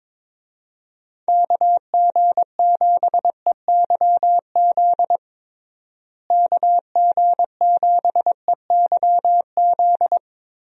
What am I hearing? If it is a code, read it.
Morse code, heard as a keyed tone, about 22 wpm, 700 Hz. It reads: KG7EYZ KG7EYZ
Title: USA Call Signs - 6 Characters - Repeated 3x - 22wpm